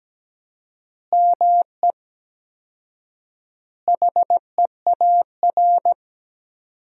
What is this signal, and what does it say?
Morse code, heard as a keyed tone, about 17 wpm, 700 Hz. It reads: ME HEAR
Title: 4x Word-Spacing - Sentences from Top 300 Words - 17wpm